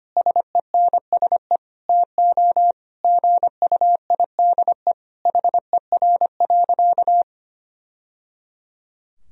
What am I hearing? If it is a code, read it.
Morse code, heard as a keyed tone, about 25 wpm, 700 Hz. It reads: SENSE TO GUIDE HER.
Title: Sentences from Top 1000 Words - Review Part 4 - 25wpm